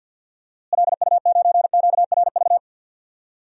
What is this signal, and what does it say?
Morse code, heard as a keyed tone, about 50 wpm, 700 Hz. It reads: LU9QRV
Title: DXCC-100 Call Signs - 6 Characters - Repeated 2x - 50wpm